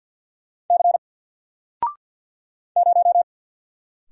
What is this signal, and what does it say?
Morse code, heard as a keyed tone, about 50 wpm, 700 Hz. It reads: C 0
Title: Single Character (Letter:Number:Punctuation:Prosign) Alt 50wpm